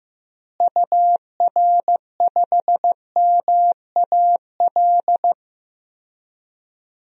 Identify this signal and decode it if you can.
Morse code, heard as a keyed tone, about 16 wpm, 700 Hz. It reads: UR5MAL
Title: DXCC-100 Call Signs - 6 Characters - Repeated 3x - 15wpm